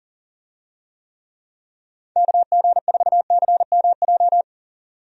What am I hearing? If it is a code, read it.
Morse code, heard as a keyed tone, about 40 wpm, 700 Hz. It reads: KG4CMJ
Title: USA Call Signs - 6 Characters - 40wpm